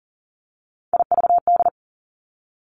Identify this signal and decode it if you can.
Morse code, heard as a keyed tone, about 41 wpm, 700 Hz. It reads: IVD
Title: Three Challenging Letter-Number Combinations 40wpm